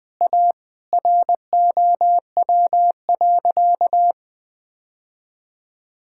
Morse code, heard as a keyed tone, about 20 wpm, 700 Hz. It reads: A ROW.
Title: Sentences with 6-Word Phrases - 20wpm